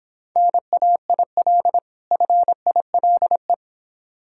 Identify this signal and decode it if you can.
Morse code, heard as a keyed tone, about 26 wpm, 700 Hz. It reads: NAIL FILE